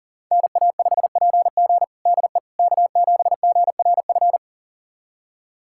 Morse code, heard as a keyed tone, about 40 wpm, 700 Hz. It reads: NA5PG DE K7GRF
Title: USA Callsign DE Callsign - 5 Characters - 40wpm